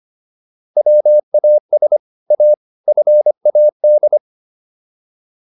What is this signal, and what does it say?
Morse code, heard as a keyed tone, about 25 wpm, 600 Hz. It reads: WAS A FAD